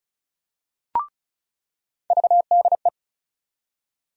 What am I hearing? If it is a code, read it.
Morse code, heard as a keyed tone, about 34 wpm, 700 Hz. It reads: VDE
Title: USA Call Sign Postfixes - 3 Letters 35wpm